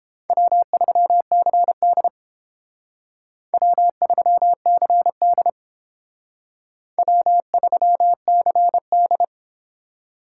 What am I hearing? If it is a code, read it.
Morse code, heard as a keyed tone, about 30 wpm, 700 Hz. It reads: W3CD W3CD W3CD